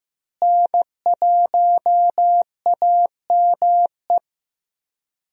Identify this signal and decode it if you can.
Morse code, heard as a keyed tone, about 15 wpm, 700 Hz. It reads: N1AME
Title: USA Call Signs - 5 Characters - Repeated 2x - 15wpm